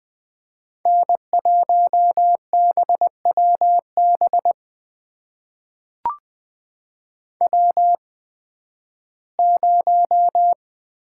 Morse code, heard as a keyed tone, about 20 wpm, spaced as for 13 wpm, 700 Hz. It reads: N1BWB W0
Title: USA Call Signs - 5 Characters - 8x Character Spacing - 20wpm